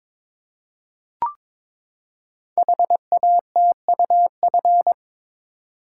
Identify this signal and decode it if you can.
Morse code, heard as a keyed tone, about 23 wpm, 700 Hz. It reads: HATUF
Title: Code Groups - 5 Letters - 22wpm